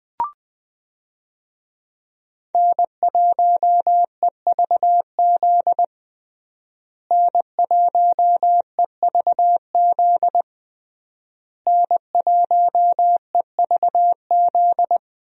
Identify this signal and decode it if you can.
Morse code, heard as a keyed tone, about 20 wpm, 700 Hz. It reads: N1EVZ N1EVZ N1EVZ
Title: USA Call Signs - 5 Characters - Repeated 3x - 20wpm